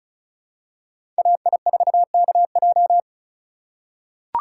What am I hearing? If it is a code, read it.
Morse code, heard as a keyed tone, about 34 wpm, 700 Hz. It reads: AI4KJ
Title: USA Call Signs - 2x Character Spacing - 35wpm